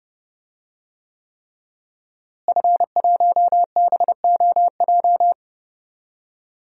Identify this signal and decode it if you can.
Morse code, heard as a keyed tone, about 30 wpm, 700 Hz. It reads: F1BOJ
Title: DXCC-100 Call Signs - 5 Characters - Repeated 3x - 30wpm